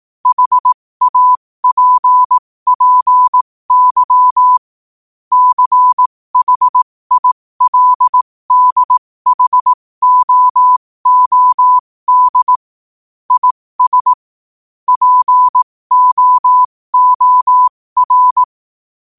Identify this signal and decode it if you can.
Morse code, heard as a keyed tone, about 18 wpm, spaced as for 15 wpm, 1000 Hz. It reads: HAPPY CHILDHOOD IS POOR
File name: qotdfiles / 15wpm